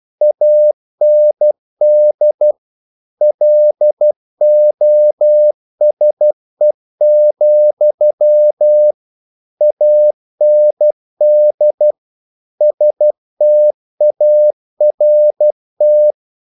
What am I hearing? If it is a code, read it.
Morse code, heard as a keyed tone, about 12 wpm, 600 Hz. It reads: AND LOSE, AND START
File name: CW / Practice / If12